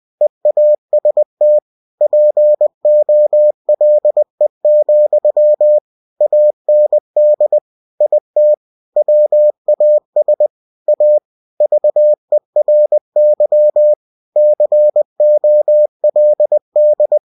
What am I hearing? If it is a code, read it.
Morse code, heard as a keyed tone, about 20 wpm, 600 Hz. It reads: EAST POLE, AND IT WAS A VERY COLD
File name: cw-books / winnie-the-pooh / winnie-20wpm-page-0093